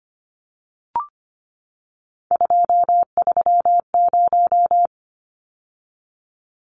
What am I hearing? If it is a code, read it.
Morse code, heard as a keyed tone, about 25 wpm, 700 Hz. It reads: 230
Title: Numbers - 3-Digit 25wpm